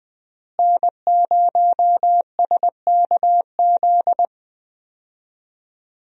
Morse code, heard as a keyed tone, about 20 wpm, 700 Hz. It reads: N0SKZ
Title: USA Call Signs - 5 Characters - Repeated 3x - 20wpm